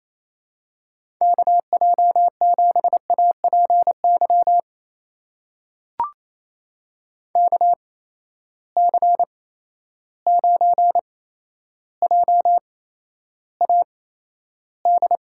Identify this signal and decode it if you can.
Morse code, heard as a keyed tone, about 28 wpm, spaced as for 12 wpm, 700 Hz. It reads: KJ7APY KC9JAD
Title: USA Call Signs - 6 Characters - 8x Character Spacing - 28wpm